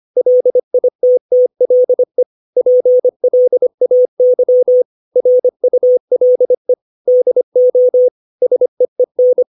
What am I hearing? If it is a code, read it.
Morse code, heard as a keyed tone, about 25 wpm, 500 Hz. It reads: LITTLE PLAY RULE DO SEEN
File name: 632 words / 632 Words 25WPM